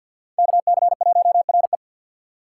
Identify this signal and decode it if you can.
Morse code, heard as a keyed tone, about 50 wpm, 700 Hz. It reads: KC1RE
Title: USA Call Signs - 5 Characters - 4x Character Spacing - 50wpm